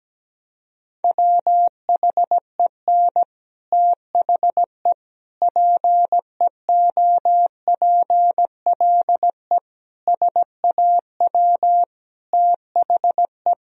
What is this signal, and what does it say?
Morse code, heard as a keyed tone, about 17 wpm, 700 Hz. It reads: WHEN THE PEOPLE SAW THE
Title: Sentences from Top 1000 Words - Review Part 1 - 17wpm